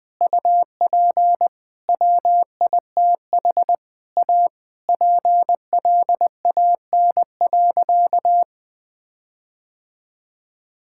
Morse code, heard as a keyed tone, about 20 wpm, 700 Hz. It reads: UP WITH A PLAN.
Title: Sentences with 5-Word Phrases - 20wpm